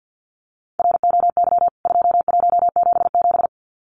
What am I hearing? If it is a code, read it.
Morse code, heard as a keyed tone, about 50 wpm, 700 Hz. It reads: ROY 2177
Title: CWT Exchange 50wpm